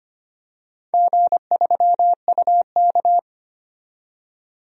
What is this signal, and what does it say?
Morse code, heard as a keyed tone, about 25 wpm, 700 Hz. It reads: G3UK
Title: DXCC-100 Call Signs - 4 Characters - 25wpm